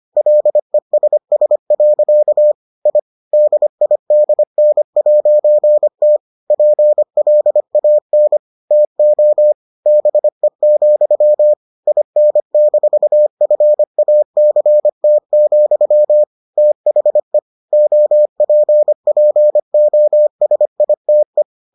Morse code, heard as a keyed tone, about 25 wpm, 600 Hz. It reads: LESS. I DIDN'T PLAN TO BE, IN-FACT, THE OPPOSITE